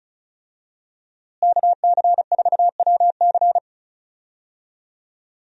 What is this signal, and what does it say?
Morse code, heard as a keyed tone, about 36 wpm, 700 Hz. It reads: KC4WC
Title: USA Call Signs - 5 Characters - Repeated 3x - 35wpm